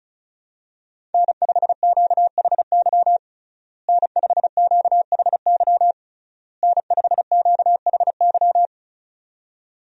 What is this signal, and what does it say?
Morse code, heard as a keyed tone, about 36 wpm, 700 Hz. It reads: N5QHY N5QHY N5QHY
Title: USA Call Signs - 5 Characters - Repeated 3x - 35wpm